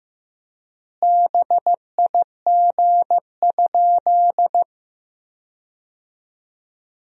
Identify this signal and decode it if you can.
Morse code, heard as a keyed tone, about 15 wpm, 700 Hz. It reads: BIG?
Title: 3x Word-Spacing - Sentences from Top 200 Words - 15wpm